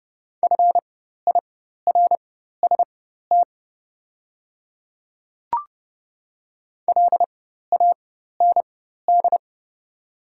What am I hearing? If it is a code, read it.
Morse code, heard as a keyed tone, about 30 wpm, spaced as for 14 wpm, 700 Hz. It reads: FIRST LAND
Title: 4x Character Spacing - Top 300 Words - 30wpm